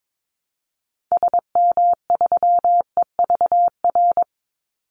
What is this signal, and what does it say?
Morse code, heard as a keyed tone, about 22 wpm, 700 Hz. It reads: SM3EVR
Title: Call Signs - 22wpm